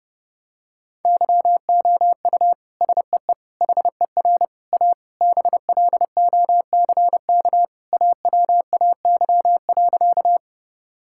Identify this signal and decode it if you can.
Morse code, heard as a keyed tone, about 31 wpm, 700 Hz. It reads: YOU SEE HER A BLOCK AWAY.
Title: Sentences from Top 1600 Words 30wpm